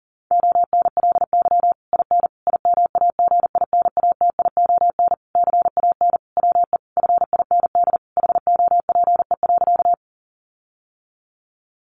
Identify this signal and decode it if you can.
Morse code, heard as a keyed tone, about 40 wpm, 700 Hz. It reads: ONLY IN IMAGINATION CAN WE FIND HOPE.
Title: Sentences from Top 1300 Words 40wpm